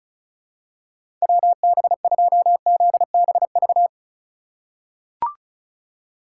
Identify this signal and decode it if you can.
Morse code, heard as a keyed tone, about 36 wpm, 700 Hz. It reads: WB2ZBV
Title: USA Call Signs - 6 Characters - 6x Character Spacing - 35wpm